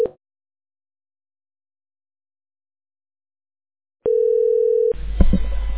Ringback tone (North America), 440 + 480 Hz.